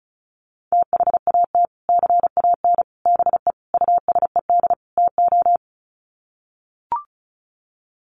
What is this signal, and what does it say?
Morse code, heard as a keyed tone, about 34 wpm, 700 Hz. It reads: THAT CAN BE USED TO
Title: Sets of 5 Words Encore 35wpm